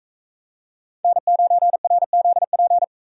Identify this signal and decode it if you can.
Morse code, heard as a keyed tone, about 40 wpm, 700 Hz. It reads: N9RZP